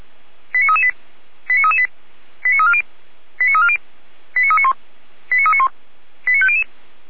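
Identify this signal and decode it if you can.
Five-tone selective call (ZVEI), heard as quick runs of five tones, tones of about 70 ms. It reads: 89338 89338 89340 89340 89391 89391 89500
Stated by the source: ZVEI five-tone call 89333 received by SDR.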